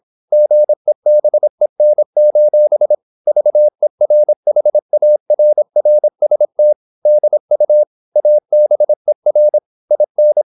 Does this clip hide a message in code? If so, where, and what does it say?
Yes: Morse code from 0.3 to 10.4 s, heard as a keyed tone, about 26 wpm, 600 Hz. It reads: GEBEN: VERHARRST DU ABER IN